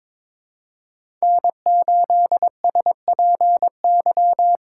Morse code, heard as a keyed tone, about 22 wpm, 700 Hz. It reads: N8SPY
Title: USA Call Signs - 5 Characters - 8x Character Spacing - 22wpm